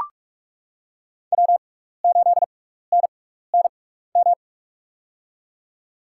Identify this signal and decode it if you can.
Morse code, heard as a keyed tone, about 44 wpm, spaced as for 15 wpm, 700 Hz. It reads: W8NNM
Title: USA Call Signs - 6x Character Spacing - 45wpm